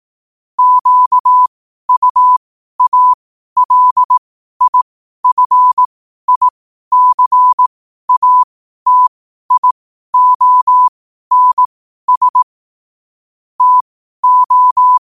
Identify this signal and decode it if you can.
Morse code, heard as a keyed tone, about 18 wpm, spaced as for 12 wpm, 1000 Hz. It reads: QUALIFICATIONS TO
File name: qotdfiles / 12wpm